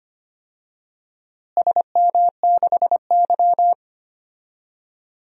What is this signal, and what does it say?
Morse code, heard as a keyed tone, about 25 wpm, 700 Hz. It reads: SM6Y
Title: DXCC-100 Call Signs - 4 Characters - Repeated 3x - 25wpm